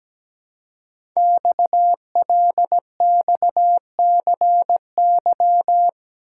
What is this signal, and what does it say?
Morse code, heard as a keyed tone, about 17 wpm, 700 Hz. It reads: XLXCY
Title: Code Groups - 5 Letters - 17wpm